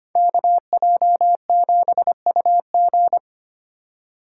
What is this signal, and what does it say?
Morse code, heard as a keyed tone, about 25 wpm, 700 Hz. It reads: KJ7UG
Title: USA Call Signs - 3x Character Spacing - 25wpm